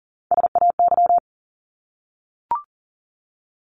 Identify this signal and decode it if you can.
Morse code, heard as a keyed tone, about 40 wpm, 700 Hz. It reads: SAY
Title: ES:S Suffix - Top 100 Words with Base Words 40wpm